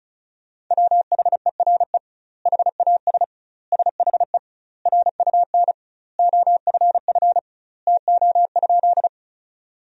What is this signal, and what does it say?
Morse code, heard as a keyed tone, about 36 wpm, 700 Hz. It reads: WHERE HAS SHE RUN OFF TO?
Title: 2x Word-Spacing - Sentences from Top 300 Words - 35wpm